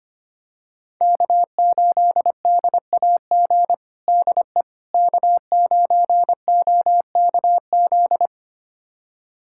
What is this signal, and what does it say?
Morse code, heard as a keyed tone, about 25 wpm, 700 Hz. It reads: K8DAG DE K9OKZ
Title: USA Callsign DE Callsign - 5 Characters - 25wpm